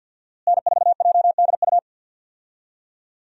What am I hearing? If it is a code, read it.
Morse code, heard as a keyed tone, about 49 wpm, 700 Hz. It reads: N4JDU
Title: USA Call Signs - 5 Characters - Repeated 2x - 50wpm